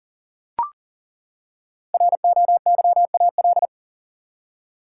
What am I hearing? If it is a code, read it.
Morse code, heard as a keyed tone, about 40 wpm, 700 Hz. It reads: ROYAL
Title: Five Letter Words - 40wpm